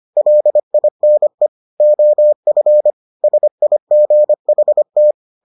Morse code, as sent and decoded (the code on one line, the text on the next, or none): .-.. .. -. / --- ..-. / ... .. --. .... -
LINE OF SIGHT